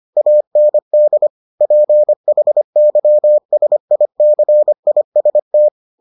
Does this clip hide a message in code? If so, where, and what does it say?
Yes: Morse code from 0.2 to 5.7 s, heard as a keyed tone, about 25 wpm, 600 Hz. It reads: AND PHYSICIST